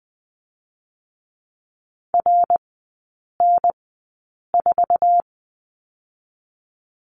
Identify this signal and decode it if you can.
Morse code, heard as a keyed tone, about 20 wpm, spaced as for 8 wpm, 700 Hz. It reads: RN4
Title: 127 Lesson 21e - Number 6 - Three Characters 20wpm